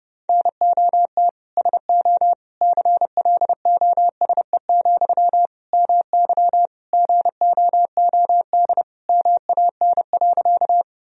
Morse code, heard as a keyed tone, about 30 wpm, 700 Hz. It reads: NOT SO CLOSE, MY GOOD MAN.
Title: Sentences from Top 1000 Words - Review Part 1 - 30wpm